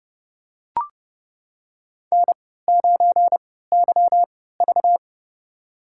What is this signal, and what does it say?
Morse code, heard as a keyed tone, about 30 wpm, spaced as for 17 wpm, 700 Hz. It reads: N9YV
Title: USA Call Signs - 3x Character Spacing - 30wpm